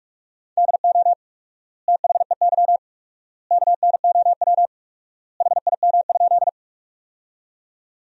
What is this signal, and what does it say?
Morse code, heard as a keyed tone, about 46 wpm, 700 Hz. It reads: DO THEY KNOW HIM?
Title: 4x Word-Spacing - Sentences from Top 100 Words - 45wpm